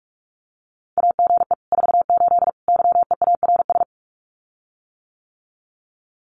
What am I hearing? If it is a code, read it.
Morse code, heard as a keyed tone, about 46 wpm, 700 Hz. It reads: AGE 48 YEARS
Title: Common QSO Phrases Encore 45wpm